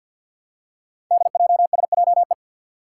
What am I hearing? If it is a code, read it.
Morse code, heard as a keyed tone, about 52 wpm, 700 Hz. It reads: DJSJE
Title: Five Challenging Letter-Number Combinations 50wpm